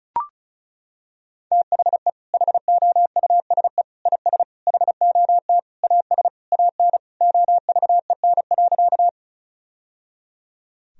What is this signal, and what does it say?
Morse code, heard as a keyed tone, about 34 wpm, 700 Hz. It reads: THE HOUSE IS HOT AS AN OVEN.
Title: Sentences from Top 1000 Words - Review Part 4 - 35wpm